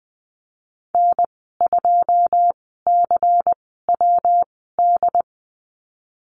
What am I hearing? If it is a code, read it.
Morse code, heard as a keyed tone, about 20 wpm, spaced as for 14 wpm, 700 Hz. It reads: N2CWD
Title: 161 Lesson 26g - Letter g - Call Signs 20wpm